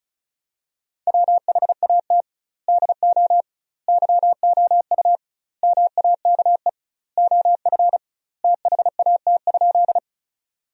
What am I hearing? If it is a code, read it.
Morse code, heard as a keyed tone, about 36 wpm, 700 Hz. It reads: WHAT DO YOU MAKE OF THAT?
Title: 2x Word-Spacing - Sentences from Top 100 Words - 35wpm